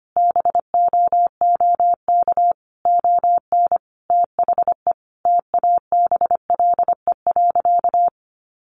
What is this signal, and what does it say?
Morse code, heard as a keyed tone, about 25 wpm, 700 Hz. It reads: BOOK ON THE TABLE.
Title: Sentences from Top 1400 Words 25wpm